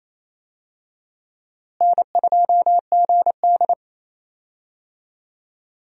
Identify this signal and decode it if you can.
Morse code, heard as a keyed tone, about 28 wpm, 700 Hz. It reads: N2GD
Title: USA Call Signs - 4 Characters - Repeated 2x - 28wpm